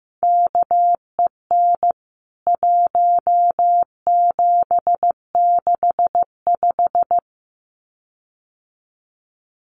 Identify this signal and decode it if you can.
Morse code, heard as a keyed tone, about 15 wpm, 700 Hz. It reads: KEN 1765
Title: CWT Exchange 15wpm